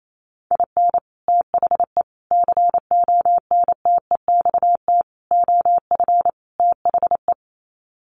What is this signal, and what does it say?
Morse code, heard as a keyed tone, about 28 wpm, 700 Hz. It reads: IN THE CONTEXT OF THE
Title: Sets of 5 Words 28wpm